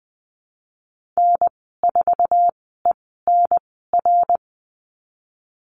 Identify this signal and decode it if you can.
Morse code, heard as a keyed tone, about 20 wpm, spaced as for 15 wpm, 700 Hz. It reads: N4ENR
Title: 055 Lesson 10g - Letter r - Call Signs 20wpm